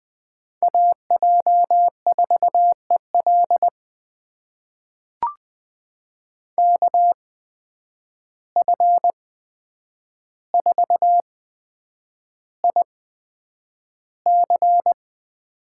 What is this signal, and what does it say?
Morse code, heard as a keyed tone, about 21 wpm, spaced as for 9 wpm, 700 Hz. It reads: AJ4EL KF4IC
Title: USA Call Signs - 5 Characters - 8x Character Spacing - 20wpm